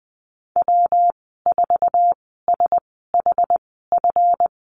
Morse code, heard as a keyed tone, about 20 wpm, spaced as for 15 wpm, 700 Hz. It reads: W4SHF
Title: 161 Lesson 26g - Letter g - Call Signs 20wpm